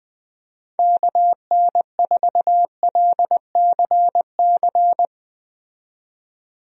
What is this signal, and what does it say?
Morse code, heard as a keyed tone, about 20 wpm, 700 Hz. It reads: KN4LCC
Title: USA Call Signs - 6 Characters - 20wpm